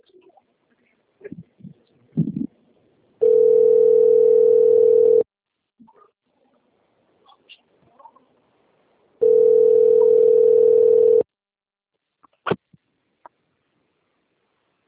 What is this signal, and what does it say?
Ringback tone (North America), 440 + 480 Hz.
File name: rg-1600-unknown- - - .88339